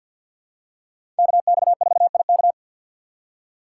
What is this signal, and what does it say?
Morse code, heard as a keyed tone, about 47 wpm, 700 Hz. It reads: KX4IK
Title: USA Call Signs - 4x Character Spacing - 50wpm